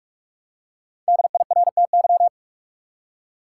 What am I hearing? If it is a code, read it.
Morse code, heard as a keyed tone, about 46 wpm, 700 Hz. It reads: DIRTY
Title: Five Letter Words - 45wpm